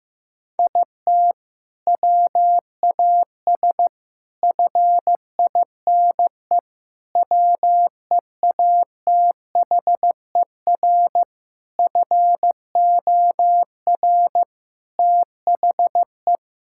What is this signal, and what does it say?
Morse code, heard as a keyed tone, about 15 wpm, 700 Hz. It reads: IT WAS FINE WEATHER FOR THE
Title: Sentences from Top 1000 Words - Review Part 5 - 15wpm